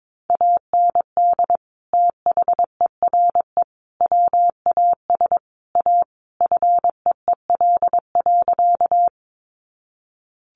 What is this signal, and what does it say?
Morse code, heard as a keyed tone, about 22 wpm, 700 Hz. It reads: AND THERE WAS A FEEL.
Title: A Sound Of Thunder 22wpm